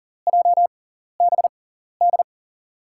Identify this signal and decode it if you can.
Morse code, heard as a keyed tone, about 38 wpm, spaced as for 14 wpm, 700 Hz. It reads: JBD
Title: USA Call Signs - 5 Characters - 6x Character Spacing - 40wpm